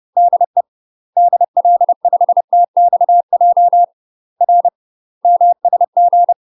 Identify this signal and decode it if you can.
Morse code, heard as a keyed tone, about 30 wpm, 700 Hz. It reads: DE DL5TXJ R MSG
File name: UK MASTER QSOs TRAINING / 30WPM / UK MASTER QSOs 0039